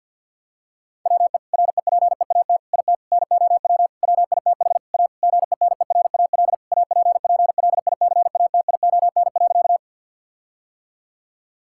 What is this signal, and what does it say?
Morse code, heard as a keyed tone, about 50 wpm, 700 Hz. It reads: WE REPEAT IT NOW WITH A GENERAL APPLICATION.
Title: Sentences from Top 1600 Words 50wpm